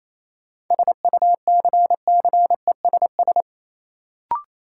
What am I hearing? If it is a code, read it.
Morse code, heard as a keyed tone, about 28 wpm, 700 Hz. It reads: SUCCESS
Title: Sentences from Top 900 Words - New Word Drill - 28wpm